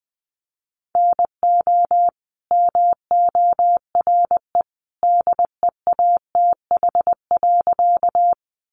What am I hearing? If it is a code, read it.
Morse code, heard as a keyed tone, about 20 wpm, 700 Hz. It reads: NO MORE DEATH.
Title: Sentences from Top 1100 Words 20wpm